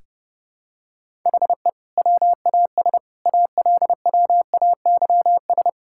Morse code, heard as a keyed tone, about 32 wpm, 700 Hz. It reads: HE WAS ALWAYS
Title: Sentences from Top 1000 Words - Review Part 6 - 30wpm